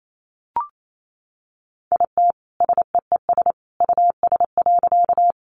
Morse code, heard as a keyed tone, about 28 wpm, 700 Hz. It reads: IT SEES US.
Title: A Sound Of Thunder 28wpm